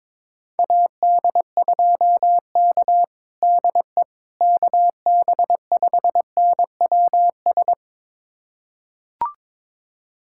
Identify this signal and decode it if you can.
Morse code, heard as a keyed tone, about 22 wpm, 700 Hz. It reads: AD2K DE KB5NWS
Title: USA Callsign DE Callsign 22wpm